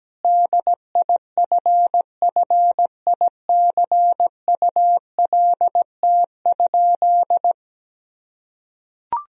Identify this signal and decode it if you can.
Morse code, heard as a keyed tone, about 17 wpm, 700 Hz. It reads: DIFFICULT?
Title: Sentences from Top 1000 Words - Review Part 4 - 17wpm